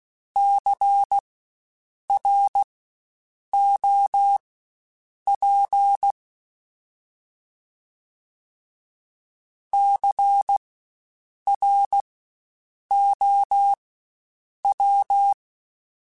Morse code, heard as a keyed tone, about 16 wpm, spaced as for 7 wpm, 800 Hz. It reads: CROP CROW
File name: Spaced Alphas / Spaced Alphas mp3s / Cs sp @15